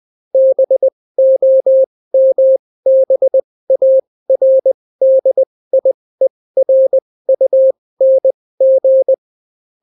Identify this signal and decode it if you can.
Morse code, heard as a keyed tone, about 20 wpm, spaced as for 16 wpm, 530 Hz. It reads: BOMBARDIERUNG